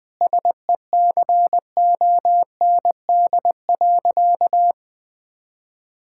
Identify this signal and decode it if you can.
Morse code, heard as a keyed tone, about 21 wpm, 700 Hz. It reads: SECOND.
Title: Difficult Sentences with Hints - 20wpm